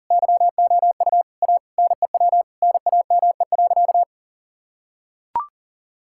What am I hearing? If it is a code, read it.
Morse code, heard as a keyed tone, about 37 wpm, 700 Hz. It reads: YOU A NEW NAME.
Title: Sentences from Top 1000 Words - Review Part 5 - 40wpm